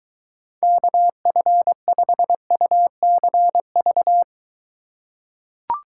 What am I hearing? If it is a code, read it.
Morse code, heard as a keyed tone, about 22 wpm, 700 Hz. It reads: KF5UCV